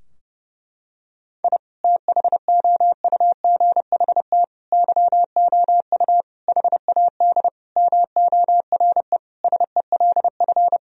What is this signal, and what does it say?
Morse code, heard as a keyed tone, about 30 wpm, 700 Hz. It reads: I THOUGHT YOU HAD MORE SELF
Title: Sentences from Top 1000 Words - Review Part 3 - 30wpm